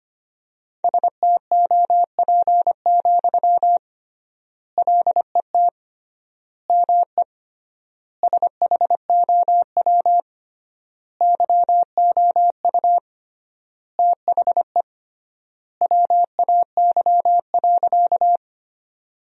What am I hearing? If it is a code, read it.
Morse code, heard as a keyed tone, about 25 wpm, 700 Hz. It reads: STOP, LET ME SHOW YOU THE WAY.
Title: 3x Word-Spacing - Sentences from Top 300 Words - 25wpm